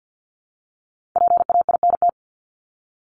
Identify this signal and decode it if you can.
Morse code, heard as a keyed tone, about 50 wpm, 700 Hz. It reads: PAINT
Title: Top 751-1000 Words 50wpm